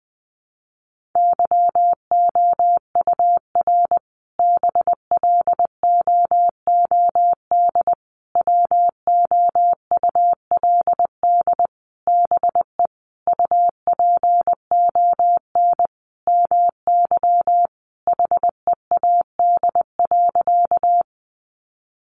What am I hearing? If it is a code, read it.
Morse code, heard as a keyed tone, about 20 wpm, 700 Hz. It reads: YOUR BLOOD WOULD BE UPON MY HEAD.
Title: Sentences from Top 1100 Words 20wpm